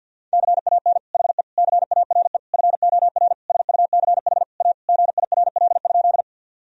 Morse code, heard as a keyed tone, about 50 wpm, 700 Hz. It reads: CAN HE CARE FOR SUCH A GIRL?
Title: Sentences from Top 1000 Words - Review Part 1 - 50wpm